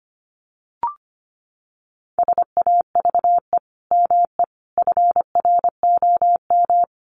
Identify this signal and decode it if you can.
Morse code, heard as a keyed tone, about 25 wpm, 700 Hz. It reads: SAVE ME FROM
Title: Sentences from Top 1100 Words 25wpm